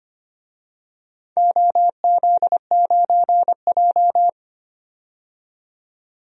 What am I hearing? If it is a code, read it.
Morse code, heard as a keyed tone, about 25 wpm, 700 Hz. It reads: OZ9J
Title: DXCC-100 Call Signs - 4 Characters - 25wpm